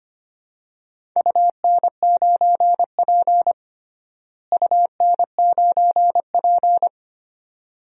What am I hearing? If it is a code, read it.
Morse code, heard as a keyed tone, about 25 wpm, 700 Hz. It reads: UN9P UN9P
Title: DXCC-100 Call Signs - 4 Characters - Repeated 2x - 25wpm